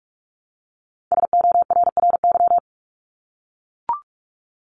Morse code, heard as a keyed tone, about 44 wpm, 700 Hz. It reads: SORRY
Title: Top 1751-2000 Words 45wpm